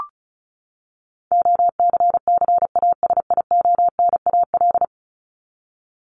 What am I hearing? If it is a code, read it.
Morse code, heard as a keyed tone, about 34 wpm, 700 Hz. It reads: OCCASIONAL
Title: LY Suffix - Top 100 Words with Base Words 35wpm